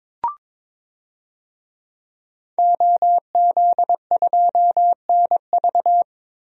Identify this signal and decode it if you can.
Morse code, heard as a keyed tone, about 22 wpm, 700 Hz. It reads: OZ2NV
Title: DXCC-100 Call Signs - 5 Characters - 22wpm